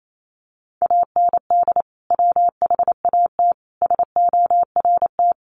Morse code, heard as a keyed tone, about 28 wpm, 700 Hz. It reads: AND WHAT SORT
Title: Sentences from Top 1300 Words 28wpm